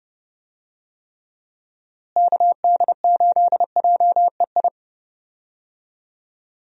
Morse code, heard as a keyed tone, about 30 wpm, 700 Hz. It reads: KD8JEI
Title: USA Call Signs - 6 Characters - Repeated 3x - 30wpm